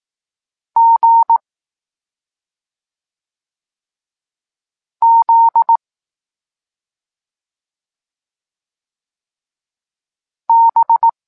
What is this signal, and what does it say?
Morse code, heard as a keyed tone, about 18 wpm, 920 Hz. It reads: G Z B
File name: Tekst-26 1